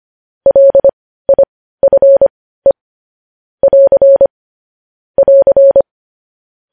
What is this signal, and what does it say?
Morse code, heard as a keyed tone, about 25 wpm, spaced as for 15 wpm, 550 Hz. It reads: LIFE <AR> <AR>
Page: news headlines in morse code at 15 wpm